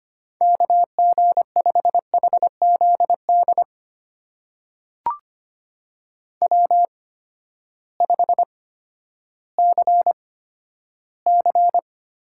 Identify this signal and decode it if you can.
Morse code, heard as a keyed tone, about 25 wpm, spaced as for 12 wpm, 700 Hz. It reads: KG5HZD W5CC
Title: USA Call Signs - 8x Character Spacing - 25wpm